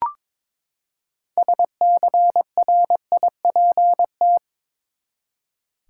Morse code, heard as a keyed tone, about 21 wpm, 700 Hz. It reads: SCRIPT
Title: Six Letter Words - 22wpm